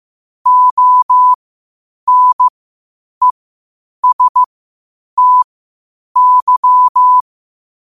Morse code, heard as a keyed tone, about 15 wpm, spaced as for 8 wpm, 1000 Hz. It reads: ONESTY
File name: qotdfiles / 8wpm